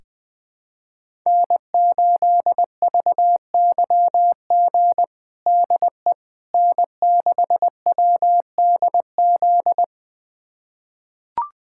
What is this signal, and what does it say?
Morse code, heard as a keyed tone, about 20 wpm, 700 Hz. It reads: N8VYG DE N6WDZ
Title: USA Callsign DE Callsign - 5 Characters - 20wpm